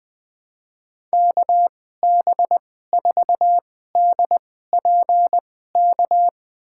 Morse code, heard as a keyed tone, about 20 wpm, spaced as for 14 wpm, 700 Hz. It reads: KB4DPK
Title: USA Call Signs - 6 Characters - 2x Character Spacing - 20wpm